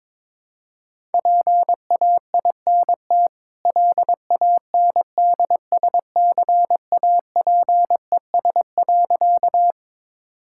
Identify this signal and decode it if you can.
Morse code, heard as a keyed tone, about 22 wpm, 690 Hz. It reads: PAINT LANDSCAPES.
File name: practice / 22wpm / Easy Sentences with Hints - 22wpm